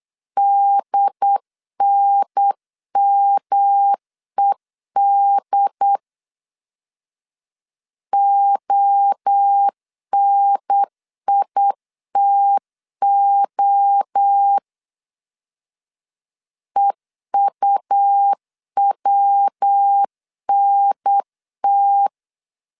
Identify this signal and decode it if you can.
Morse code, heard as a keyed tone, about 8 wpm, 800 Hz. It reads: DNMED ONITO EUWNT